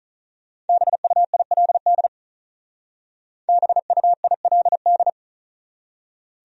Morse code, heard as a keyed tone, about 46 wpm, 700 Hz. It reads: BUILD BUILD